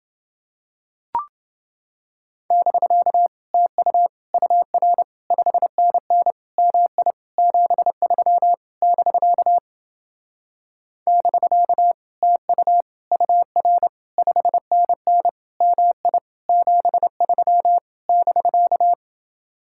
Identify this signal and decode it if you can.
Morse code, heard as a keyed tone, about 29 wpm, 700 Hz. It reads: <BK> TU UR 5NN MI 73 <BK> <BK> TU UR 5NN MI 73 <BK>